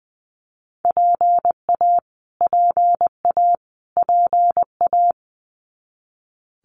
Morse code, heard as a keyed tone, about 20 wpm, 700 Hz. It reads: PA PA PA
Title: US States Repeated 3x 20wpm